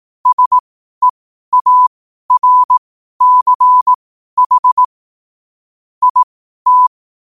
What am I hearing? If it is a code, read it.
Morse code, heard as a keyed tone, about 18 wpm, spaced as for 12 wpm, 1000 Hz. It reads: SEARCH IT
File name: qotdfiles / 12wpm